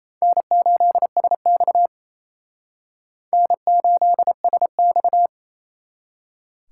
Morse code, heard as a keyed tone, about 29 wpm, 700 Hz. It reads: N8SX N8SX
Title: Speed Racing - USA 1x2 Call Signs 28wpm